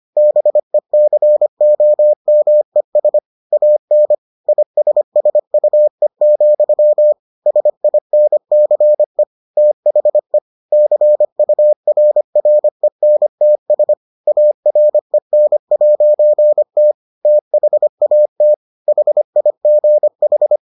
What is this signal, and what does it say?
Morse code, heard as a keyed tone, about 25 wpm, 600 Hz. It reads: BECOMES AN ISSUE, SINCE THE CURRENTS AREN'T THAT HIGH